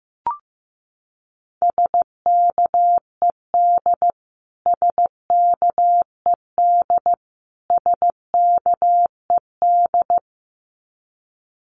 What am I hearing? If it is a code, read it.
Morse code, heard as a keyed tone, about 15 wpm, 700 Hz. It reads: SKED SKED SKED
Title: Abbreviations Repeated 3x - 15wpm